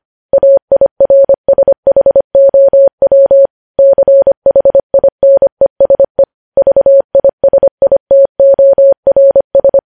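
Morse code, heard as a keyed tone, about 25 wpm, 550 Hz. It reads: AIRSHOW CHINESE VISITORS
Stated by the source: News headlines sent at 25.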